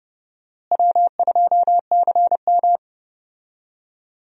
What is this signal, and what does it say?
Morse code, heard as a keyed tone, about 30 wpm, 700 Hz. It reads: W2CM
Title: USA Call Signs - 4 Characters - Repeated 2x - 30wpm